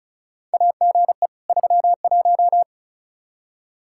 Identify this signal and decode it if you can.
Morse code, heard as a keyed tone, about 36 wpm, 700 Hz. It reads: AGE 31
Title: Practice QSO - Exchange 3 - More Info - 35wpm